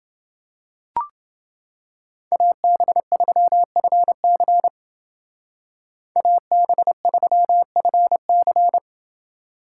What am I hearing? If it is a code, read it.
Morse code, heard as a keyed tone, about 28 wpm, 700 Hz. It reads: AB3FC AB3FC